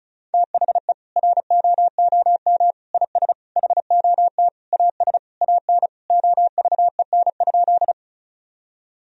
Morse code, heard as a keyed tone, about 36 wpm, 700 Hz. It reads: THE ROOM IS HOT AS AN OVEN?
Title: Sentences from Top 1000 Words - Review Part 4 - 35wpm